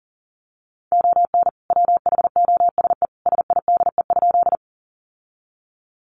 Morse code, heard as a keyed tone, about 40 wpm, 700 Hz. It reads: ON WHOSE SIDE?
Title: Sentences from Top 1200 Words 40wpm